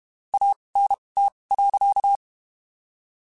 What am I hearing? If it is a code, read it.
Morse code, heard as a keyed tone, about 32 wpm, spaced as for 23 wpm, 800 Hz. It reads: ANT.
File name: (spaced) / QSOs Spaced-mp3s / WD0FBT 1 @30 sp